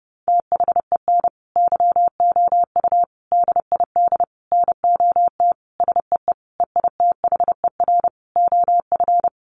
Morse code, heard as a keyed tone, about 28 wpm, 700 Hz. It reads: THEN YOU DID NOT SEE EITHER OF
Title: Sentences from Top 600 Words 30wpm